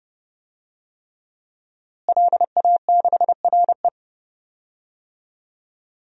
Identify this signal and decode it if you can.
Morse code, heard as a keyed tone, about 32 wpm, 700 Hz. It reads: LA6RE
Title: DXCC-100 Call Signs - 5 Characters - Repeated 3x - 30wpm